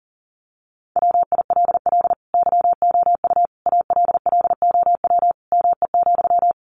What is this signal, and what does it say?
Morse code, heard as a keyed tone, about 37 wpm, 700 Hz. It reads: WILL YOU ALLOW ME,
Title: Sentences from Top 700 Words 40wpm